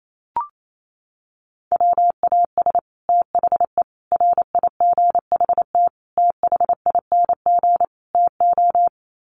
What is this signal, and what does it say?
Morse code, heard as a keyed tone, about 28 wpm, 700 Hz. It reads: WAS THE RIGHT THING TO
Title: Sets of 5 Words Encore2 28wpm